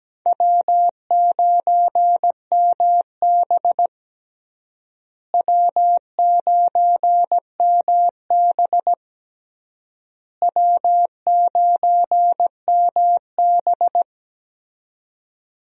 Morse code, heard as a keyed tone, about 17 wpm, 700 Hz. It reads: W9MB W9MB W9MB
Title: USA Call Signs - 4 Characters - Repeated 3x - 17wpm